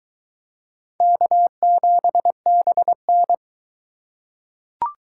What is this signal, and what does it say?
Morse code, heard as a keyed tone, about 22 wpm, 700 Hz. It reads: K7BN